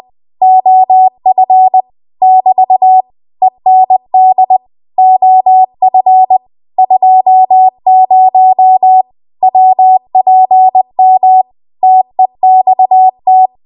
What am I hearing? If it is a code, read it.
Morse code, heard as a keyed tone, about 20 wpm, 750 Hz. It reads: OF <BT> END OF 20 WPM TEXT
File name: Morse / 20WPM